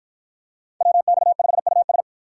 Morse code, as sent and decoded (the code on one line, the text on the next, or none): .-- -..- ..... ..- ...
WX5US